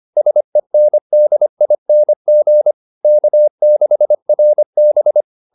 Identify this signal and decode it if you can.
Morse code, heard as a keyed tone, about 25 wpm, 600 Hz. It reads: SENDING K6RB